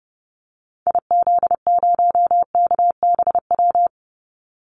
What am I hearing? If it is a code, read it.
Morse code, heard as a keyed tone, about 30 wpm, 700 Hz. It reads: IZ0KBW
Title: Call Signs Repeated 2x - 30wpm